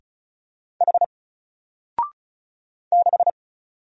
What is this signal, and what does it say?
Morse code, heard as a keyed tone, about 34 wpm, 700 Hz. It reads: H 6
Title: Single Character (Letter:Number:Punctuation:Prosign) 35wpm